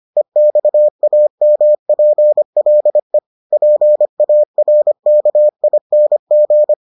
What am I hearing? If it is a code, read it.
Morse code, heard as a keyed tone, about 25 wpm, 600 Hz. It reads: EXAMPLE PARKING